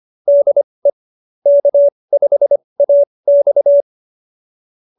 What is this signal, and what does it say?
Morse code, heard as a keyed tone, about 25 wpm, spaced as for 20 wpm, 590 Hz. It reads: DE K5AX